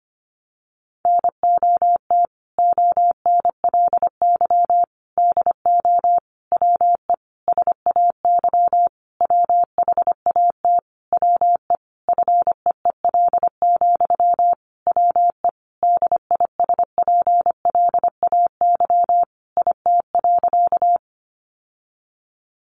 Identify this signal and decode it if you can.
Morse code, heard as a keyed tone, about 25 wpm, 700 Hz. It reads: NOT ONLY DO WE SAY WHAT WE FEEL, WE DISPLAY IT.
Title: Sentences from Top 1200 Words 25wpm